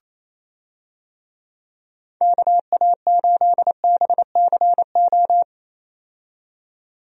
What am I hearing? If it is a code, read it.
Morse code, heard as a keyed tone, about 27 wpm, 700 Hz. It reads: KA8BCO
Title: USA Call Signs - 6 Characters - 28wpm